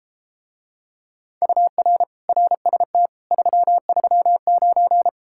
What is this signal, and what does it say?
Morse code, heard as a keyed tone, about 33 wpm, 700 Hz. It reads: UR RST 339